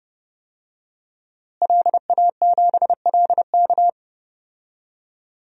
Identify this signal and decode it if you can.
Morse code, heard as a keyed tone, about 32 wpm, 700 Hz. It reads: LA7LK
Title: DXCC-100 Call Signs - 5 Characters - Repeated 2x - 30wpm